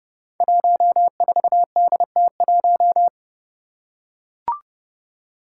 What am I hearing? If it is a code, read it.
Morse code, heard as a keyed tone, about 30 wpm, 700 Hz. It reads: 14DT1
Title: Five Challenging Letter-Number Combinations 30wpm